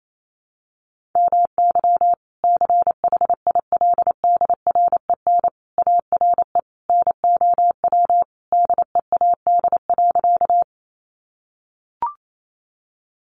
Morse code, heard as a keyed tone, about 28 wpm, 700 Hz. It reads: MY CHILDREN ARE NOW DEAD.
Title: Sentences from Top 1500 Words 28wpm